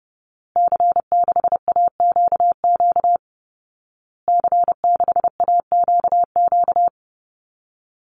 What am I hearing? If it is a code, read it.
Morse code, heard as a keyed tone, about 30 wpm, 700 Hz. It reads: C6AQQ C6AQQ
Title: Call Signs Repeated 2x - 30wpm